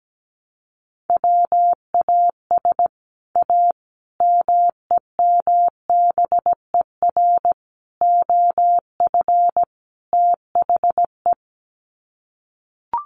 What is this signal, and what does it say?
Morse code, heard as a keyed tone, about 17 wpm, 700 Hz. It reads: WAS A MEMBER OF THE
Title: Sets of 5 Words 17wpm